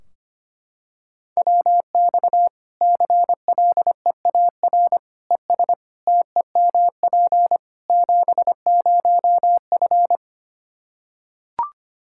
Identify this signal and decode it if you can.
Morse code, heard as a keyed tone, about 25 wpm, 700 Hz. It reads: WX CLEAR ES TEMP 70F
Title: Practice QSO - Exchange 2 - Info - 25wpm